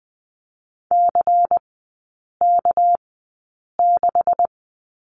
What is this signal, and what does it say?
Morse code, heard as a keyed tone, about 20 wpm, spaced as for 8 wpm, 700 Hz. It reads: CK6
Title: 198 Lesson 32e - Letter k - Three Characters 20wpm